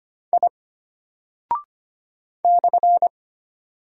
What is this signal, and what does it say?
Morse code, heard as a keyed tone, about 25 wpm, 700 Hz. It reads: I /
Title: Single Character (Letter:Number:Punctuation:Prosign) Alt 25wpm